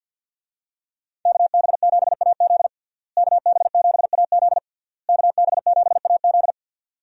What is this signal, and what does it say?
Morse code, heard as a keyed tone, about 45 wpm, 700 Hz. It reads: KB7AZ KB7AZ KB7AZ
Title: USA Call Signs - 5 Characters - Repeated 3x - 50wpm